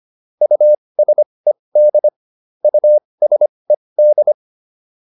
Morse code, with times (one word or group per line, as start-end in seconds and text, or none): 0.4-2.1 USED
2.6-4.3 USED